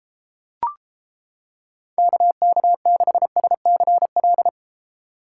Morse code, heard as a keyed tone, about 31 wpm, 700 Hz. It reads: KK6SCL